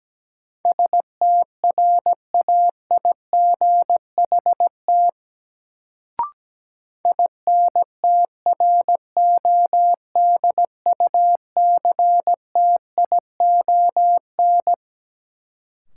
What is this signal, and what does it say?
Morse code, heard as a keyed tone, about 17 wpm, 700 Hz. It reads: STRAIGHT INTRODUCTION
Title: Sentences from Top 1000 Words - New Word Drill - 17wpm